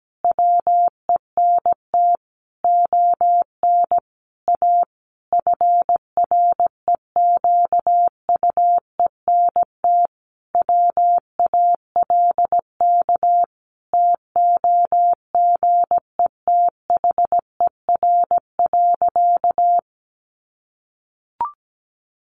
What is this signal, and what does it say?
Morse code, heard as a keyed tone, about 17 wpm, 700 Hz. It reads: WENT ON A FREQUENT WALK TOGETHER.
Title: Sentences from Top 1400 Words 17wpm